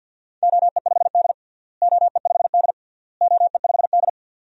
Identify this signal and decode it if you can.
Morse code, heard as a keyed tone, about 48 wpm, 700 Hz. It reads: OE5D OE5D OE5D
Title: DXCC-100 Call Signs - 4 Characters - Repeated 3x - 50wpm